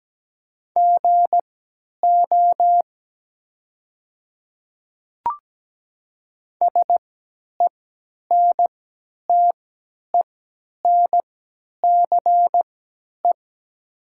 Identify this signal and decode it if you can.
Morse code, heard as a keyed tone, about 17 wpm, spaced as for 10 wpm, 700 Hz. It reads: GO SENTENCE
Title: 3x Character Spacing - Top 200 Words - 17wpm